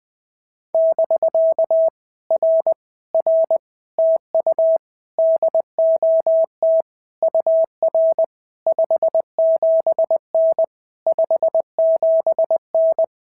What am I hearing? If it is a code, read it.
Morse code, heard as a keyed tone, about 20 wpm, 650 Hz. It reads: <BK> R R TU DOT UR 57N 57N
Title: POTA QSO Practice - 20wpm